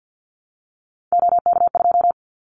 Morse code, heard as a keyed tone, about 50 wpm, 700 Hz. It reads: OK2
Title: Callsign Prefixes for Top 50 DXCC - 50wpm